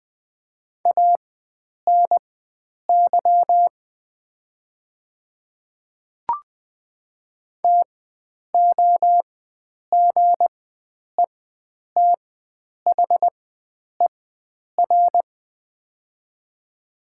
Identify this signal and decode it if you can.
Morse code, heard as a keyed tone, about 20 wpm, spaced as for 9 wpm, 700 Hz. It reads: ANY TOGETHER
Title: 4x Character Spacing - Top 300 Words - 20wpm